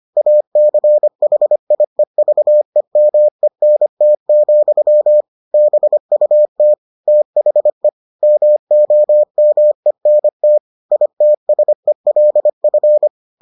Morse code, heard as a keyed tone, about 25 wpm, 600 Hz. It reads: ACHIEVEMENT, BUT THE MOMENT ITSELF